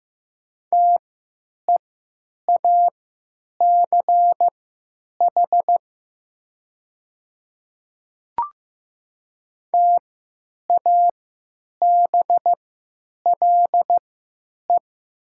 Morse code, heard as a keyed tone, about 15 wpm, spaced as for 9 wpm, 700 Hz. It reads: TEACH TABLE
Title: 3x Character Spacing - Top 500 Words - 15wpm